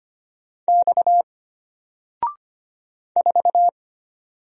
Morse code, heard as a keyed tone, about 25 wpm, 700 Hz. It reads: X 4
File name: Single-Letter-Number-25wpm